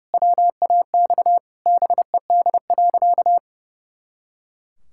Morse code, heard as a keyed tone, about 30 wpm, 700 Hz. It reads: WAX BED.
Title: Pangram Challenge v2 - 30wpm